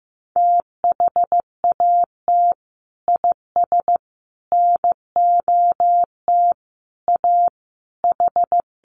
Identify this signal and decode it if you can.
Morse code, heard as a keyed tone, about 15 wpm, 700 Hz. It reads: THAT IS NOT A H
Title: Sentences from Top 600 Words 15wpm